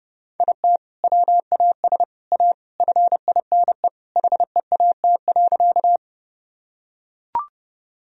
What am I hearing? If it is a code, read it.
Morse code, heard as a keyed tone, about 30 wpm, 700 Hz. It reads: IT WAS A FINE HEAT.
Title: Sentences from Top 1000 Words - Review Part 3 - 30wpm